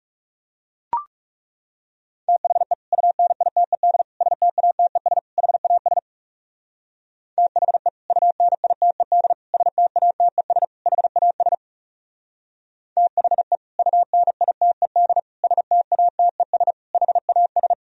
Morse code, heard as a keyed tone, about 36 wpm, 700 Hz. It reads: THE UNITED STATES HAS THE UNITED STATES HAS THE UNITED STATES HAS
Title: Speed-Racing - Sets of 4 Words 30wpm